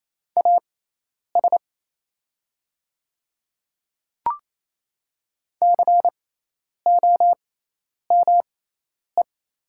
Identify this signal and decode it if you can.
Morse code, heard as a keyed tone, about 28 wpm, spaced as for 10 wpm, 700 Hz. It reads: AS COME
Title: 6x Character Spacing - Top 100 Words - 28wpm